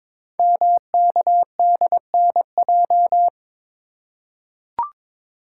Morse code, heard as a keyed tone, about 22 wpm, 700 Hz. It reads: MKDNJ
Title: Code Groups - 5 Letters - 22wpm